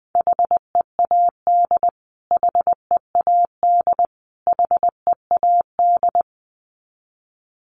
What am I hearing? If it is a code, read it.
Morse code, heard as a keyed tone, about 20 wpm, 700 Hz. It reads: HEAD HEAD HEAD
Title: Top 500 Words Repeated 20wpm